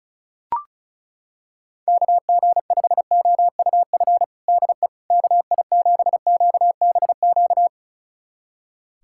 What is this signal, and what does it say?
Morse code, heard as a keyed tone, about 34 wpm, 700 Hz. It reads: KG5OUF DE KI7QBQ
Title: USA Callsign DE Callsign 35wpm